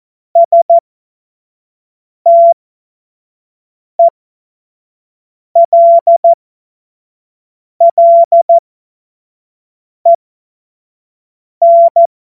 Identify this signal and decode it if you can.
Morse code, heard as a keyed tone, about 14 wpm, spaced as for 5 wpm, 680 Hz. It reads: STELLEN
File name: Morsekurs / Witze / Witz 25. Juli 2023 25 F70